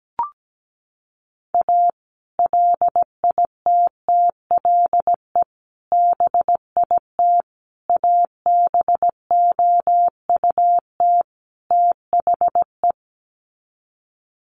Morse code, heard as a keyed tone, about 17 wpm, 700 Hz. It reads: A LITTLE BIT ABOUT THE
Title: Sets of 5 Words Encore2 17wpm